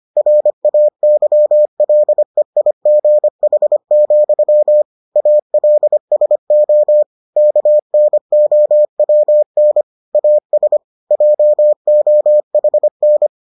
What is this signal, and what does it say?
Morse code, heard as a keyed tone, about 25 wpm, 600 Hz. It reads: RAYLEIGH, ALSO KNOWN AS JOHN